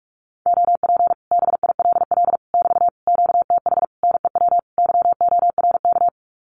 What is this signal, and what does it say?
Morse code, heard as a keyed tone, about 44 wpm, 700 Hz. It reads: OP BILL <BT> QTH NEW YORK
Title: Common QSO Phrases Encore 45wpm